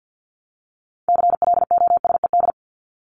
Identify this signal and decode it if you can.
Morse code, heard as a keyed tone, about 50 wpm, 700 Hz. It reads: CLOSED
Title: Top 1251-1500 Words 50wpm